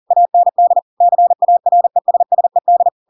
Morse code, heard as a keyed tone, about 40 wpm, 700 Hz. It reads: AND CARESSED
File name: Pinocchio / Pinocchio ch7 40 40 ik1hge